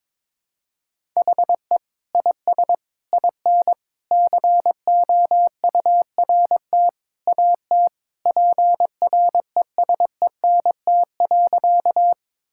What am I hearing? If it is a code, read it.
Morse code, heard as a keyed tone, about 22 wpm, 700 Hz. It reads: HE IS IN COURT AT PRESENT.
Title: Sentences from Top 1600 Words 22wpm